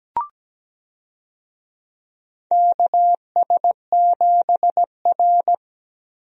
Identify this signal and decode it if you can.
Morse code, heard as a keyed tone, about 17 wpm, 700 Hz. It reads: KS7R
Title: USA Call Signs - 4 Characters - Repeated 3x - 17wpm